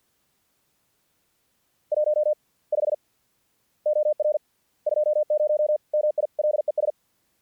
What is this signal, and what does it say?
Morse code, heard as a keyed tone, about 50 wpm, spaced as for 38 wpm, 600 Hz. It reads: 15 OR 20 MILES
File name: archived / CRSnet